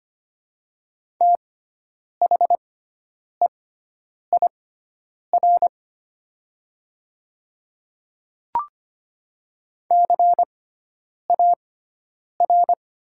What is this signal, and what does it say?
Morse code, heard as a keyed tone, about 25 wpm, spaced as for 9 wpm, 700 Hz. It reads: THEIR CAR
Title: 6x Character Spacing - Top 500 Words - 25wpm